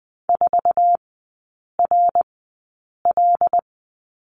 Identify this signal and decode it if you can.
Morse code, heard as a keyed tone, about 20 wpm, spaced as for 8 wpm, 700 Hz. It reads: 4RL
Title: 107 Lesson 18e - Letter m - Three Characters 20wpm